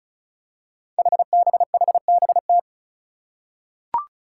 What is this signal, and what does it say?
Morse code, heard as a keyed tone, about 36 wpm, 700 Hz. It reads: HBHBT
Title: Five Challenging Letter-Number Combinations 35wpm